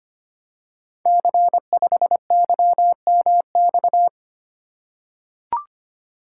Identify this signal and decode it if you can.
Morse code, heard as a keyed tone, about 25 wpm, 700 Hz. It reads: C5YMX
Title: Code Groups - 5 Letter:Numbers - 25wpm